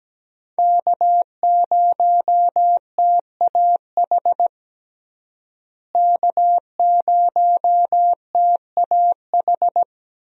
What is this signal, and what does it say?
Morse code, heard as a keyed tone, about 17 wpm, 700 Hz. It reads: K0TAH K0TAH
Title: USA Call Signs - 5 Characters - Repeated 3x - 17wpm